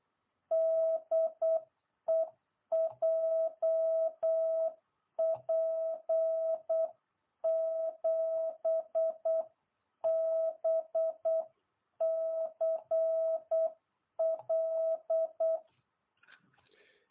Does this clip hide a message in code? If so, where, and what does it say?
Yes: Morse code from 0.5 to 15.6 s, heard as a keyed tone, about 8 wpm, 660 Hz. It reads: DE JP7BCL